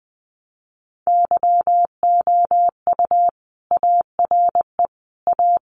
Morse code, heard as a keyed tone, about 20 wpm, 700 Hz. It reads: YOU ARE A
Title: Sentences from Top 1000 Words 20wpm